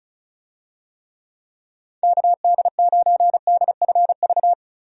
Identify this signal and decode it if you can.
Morse code, heard as a keyed tone, about 34 wpm, 700 Hz. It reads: KD9DFV
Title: USA Call Signs - 6 Characters - Repeated 3x - 35wpm